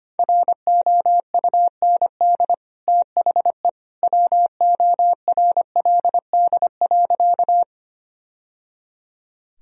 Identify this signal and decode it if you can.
Morse code, heard as a keyed tone, about 25 wpm, 700 Hz. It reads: ROUND THE WORLD.
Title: Sentences from Top 1000 Words - Review Part 1 - 25wpm